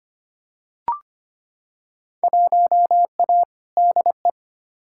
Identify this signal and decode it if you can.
Morse code, heard as a keyed tone, about 25 wpm, 700 Hz. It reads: 1A DE
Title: Field Day Exchanges Encore - 25wpm